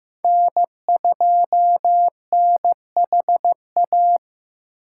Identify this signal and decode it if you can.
Morse code, heard as a keyed tone, about 15 wpm, 700 Hz. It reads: N2NHA
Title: USA Call Signs - 5 Characters - 3x Character Spacing - 15wpm